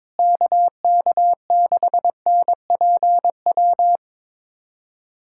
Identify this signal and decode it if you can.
Morse code, heard as a keyed tone, about 22 wpm, 700 Hz. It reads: KK6NPW
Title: USA Callsign DE Callsign - 6 Characters - 22wpm